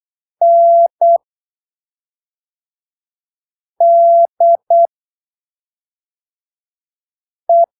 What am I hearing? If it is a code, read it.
Morse code, heard as a keyed tone, about 8 wpm, spaced as for 3 wpm, 680 Hz. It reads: NDE
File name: arquivos / texto / texto 22 08ppm